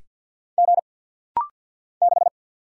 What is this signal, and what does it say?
Morse code, heard as a keyed tone, about 50 wpm, 690 Hz. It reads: G 6
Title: Single Letter-Number Alt - Rapid-Fire 50wpm